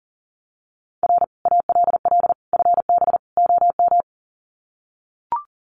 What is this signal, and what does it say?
Morse code, heard as a keyed tone, about 37 wpm, 700 Hz. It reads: R ALL FB OM
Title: Common QSO Phrases 40wpm